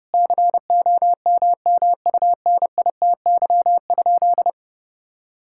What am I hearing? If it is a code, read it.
Morse code, heard as a keyed tone, about 30 wpm, 700 Hz. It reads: COMMUNITY?
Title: Sentences from Top 1000 Words - Review Part 5 - 30wpm